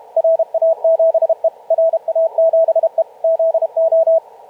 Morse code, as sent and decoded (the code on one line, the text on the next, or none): .-. .- --... / .-. .- --... / --.. ---
RA7E RA7E ZO